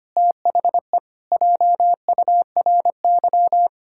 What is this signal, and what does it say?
Morse code, heard as a keyed tone, about 25 wpm, 700 Hz. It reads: THE JURY
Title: Pangram Challenge v2 - 25wpm